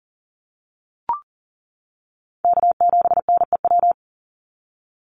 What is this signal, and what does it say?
Morse code, heard as a keyed tone, about 40 wpm, 700 Hz. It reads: K7NEW
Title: NA SOTA Chaser Callsigns 40wpm